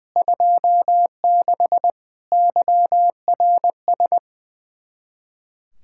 Morse code, heard as a keyed tone, about 20 wpm, 700 Hz. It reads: 26 YRS
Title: Practice QSO - Exchange 3 - More Info - 20wpm